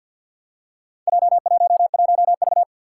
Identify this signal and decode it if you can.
Morse code, heard as a keyed tone, about 50 wpm, 700 Hz. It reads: J11V
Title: Four Challenging Letter-Number Combinations 50wpm